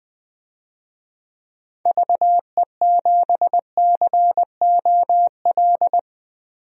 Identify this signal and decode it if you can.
Morse code, heard as a keyed tone, about 20 wpm, 700 Hz. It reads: VE7COL
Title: DXCC-100 Call Signs - 6 Characters - 20wpm